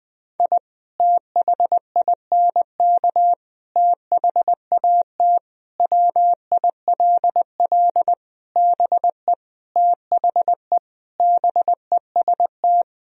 Morse code, heard as a keyed tone, about 20 wpm, 700 Hz. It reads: I THINK THAT WILL BE THE BEST
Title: Sentences from Top 1000 Words - Review Part 2 - 20wpm